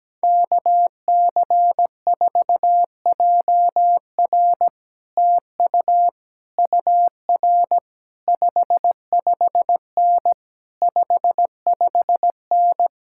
Morse code, heard as a keyed tone, about 17 wpm, 700 Hz. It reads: KC4JR TU UR 55N 55N
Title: Speed-Racing Common QSO Phrases - 17wpm